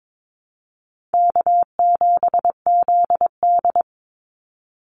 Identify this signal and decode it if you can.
Morse code, heard as a keyed tone, about 22 wpm, 700 Hz. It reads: K7ZD
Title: Easy Call Signs - 22wpm